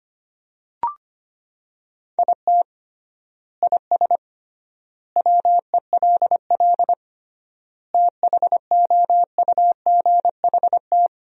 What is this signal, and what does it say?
Morse code, heard as a keyed tone, about 25 wpm, 700 Hz. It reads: IT IS WELL THOUGHT
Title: 3x Word-Spacing - Sentences from Top 300 Words - 25wpm